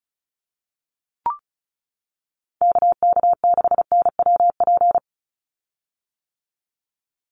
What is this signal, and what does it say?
Morse code, heard as a keyed tone, about 34 wpm, 700 Hz. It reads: KK6NWP
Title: USA Call Signs 35wpm